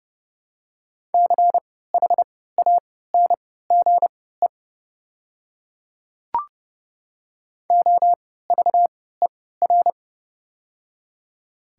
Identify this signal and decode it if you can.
Morse code, heard as a keyed tone, about 31 wpm, spaced as for 17 wpm, 700 Hz. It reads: CHANGE OVER
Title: 3x Character Spacing - Top 300 Words - 30wpm